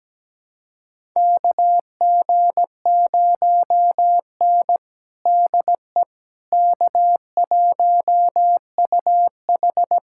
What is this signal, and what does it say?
Morse code, heard as a keyed tone, about 17 wpm, 700 Hz. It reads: KG0N DE K1UH
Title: USA Callsign DE Callsign - 4 Characters - 17wpm